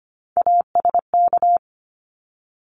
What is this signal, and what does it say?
Morse code, heard as a keyed tone, about 25 wpm, 700 Hz. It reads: ASK
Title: ED Suffix - Top 100 Words with Base Words 25wpm